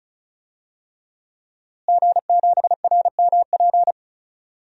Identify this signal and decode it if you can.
Morse code, heard as a keyed tone, about 36 wpm, 700 Hz. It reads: G7RMP
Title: DXCC-100 Call Signs - 5 Characters - Repeated 3x - 35wpm